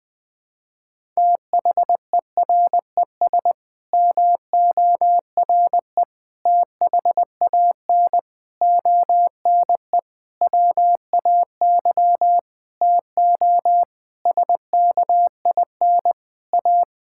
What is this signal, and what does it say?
Morse code, heard as a keyed tone, about 19 wpm, 700 Hz. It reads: THERES MORE THAN ONE WAY TO SKIN A
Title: Sets of 9 Words - 20wpm